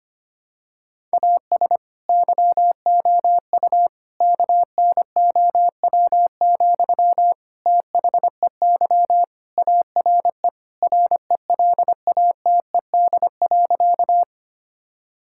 Morse code, heard as a keyed tone, about 25 wpm, 700 Hz. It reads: AS YOU KNOW, THEY ARE RELATED.
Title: Sentences from Top 1000 Words - Review Part 5 - 25wpm